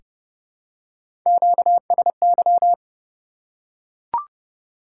Morse code, heard as a keyed tone, about 30 wpm, 700 Hz. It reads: QSY
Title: Top 180 QSO Elements 30wpm